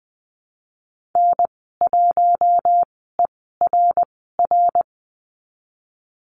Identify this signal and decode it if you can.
Morse code, heard as a keyed tone, about 20 wpm, spaced as for 15 wpm, 700 Hz. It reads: N1ERR
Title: 147 Lesson 24g - Letter y - Call Signs 20wpm